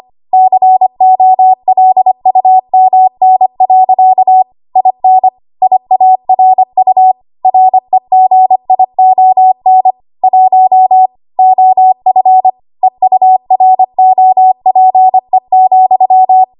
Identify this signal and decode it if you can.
Morse code, heard as a keyed tone, about 25 wpm, 750 Hz. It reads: COLUMN. IN IARU REGION 1 OF EUROPE,
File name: Archive / 25 WPM / 25WPM